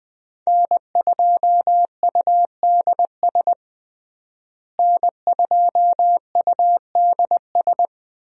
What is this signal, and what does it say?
Morse code, heard as a keyed tone, about 20 wpm, 700 Hz. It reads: N2UDS N2UDS
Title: USA Call Signs - 5 Characters - Repeated 2x - 20wpm